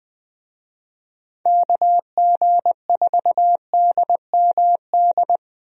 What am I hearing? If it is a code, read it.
Morse code, heard as a keyed tone, about 20 wpm, 700 Hz. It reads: KG4DMD
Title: USA Call Signs - 6 Characters - Repeated 3x - 20wpm